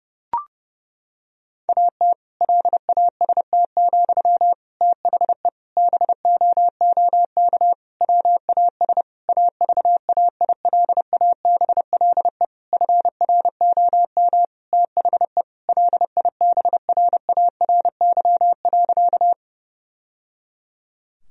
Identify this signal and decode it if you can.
Morse code, heard as a keyed tone, about 30 wpm, 700 Hz. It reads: AT LAST, THE BOOK WAS AVAILABLE FROM THE LIBRARY.
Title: Sentences from Top 1000 Words - Review Part 3 - 30wpm